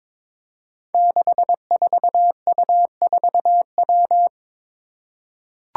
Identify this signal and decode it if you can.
Morse code, heard as a keyed tone, about 22 wpm, 700 Hz. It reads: 64U4W
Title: Five Challenging Letter-Number Combinations 22wpm